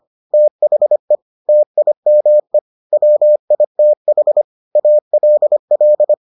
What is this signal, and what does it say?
Morse code, heard as a keyed tone, about 25 wpm, 600 Hz. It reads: THE TIME WITH ALL